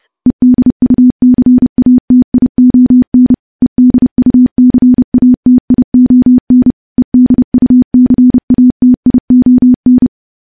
Morse code, heard as a keyed tone, about 30 wpm, 250 Hz. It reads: EDUCATION EDUCATION EDUCATION
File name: morse / morse-training-30WPM